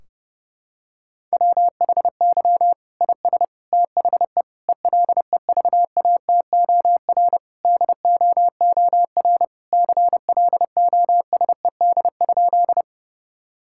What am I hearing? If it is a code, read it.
Morse code, heard as a keyed tone, about 32 wpm, 700 Hz. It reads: WHY IS THE ELEVATOR DOOR CLOSED?
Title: Sentences from Top 1600 Words 30wpm